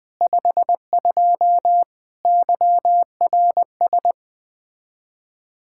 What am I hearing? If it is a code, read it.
Morse code, heard as a keyed tone, about 20 wpm, 700 Hz. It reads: 52 YRS
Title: Speed-Racing Common QSO Phrases - 17wpm